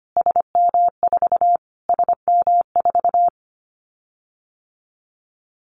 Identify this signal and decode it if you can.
Morse code, heard as a keyed tone, about 25 wpm, 700 Hz. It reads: SM4 SM4
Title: Callsign Prefixes for Top 25 DXCC - Repeated 3x - 25wpm